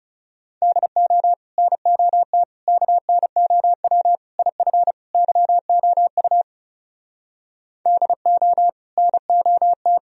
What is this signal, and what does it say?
Morse code, heard as a keyed tone, about 36 wpm, 700 Hz. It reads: DO NOT KNOW IF YOU DO NOT
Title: Speed-Racing - Sets of 5 Words 30wpm